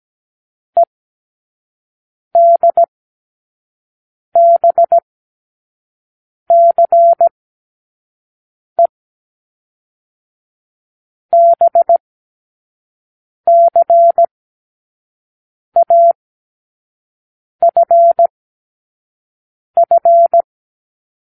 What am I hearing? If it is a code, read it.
Morse code, heard as a keyed tone, about 17 wpm, spaced as for 5 wpm, 680 Hz. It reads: EDBCE BCAFF
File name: k7qo / Lesson9(A-F)